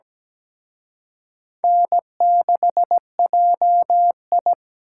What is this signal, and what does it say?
Morse code, heard as a keyed tone, about 20 wpm, 700 Hz. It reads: N6JI
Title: Speed Racing - USA 1x2 Call Signs 17wpm